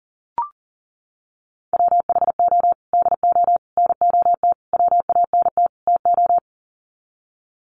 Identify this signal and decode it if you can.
Morse code, heard as a keyed tone, about 40 wpm, 700 Hz. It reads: WHO DO NOT WANT TO
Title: Sets of 5 Words Encore2 40wpm